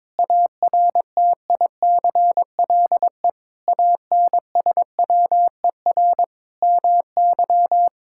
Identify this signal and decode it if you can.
Morse code, heard as a keyed tone, about 22 wpm, 700 Hz. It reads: ARTICLE ANSWER MY
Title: Sentences from Top 1000 Words - Review Part 5 - 22wpm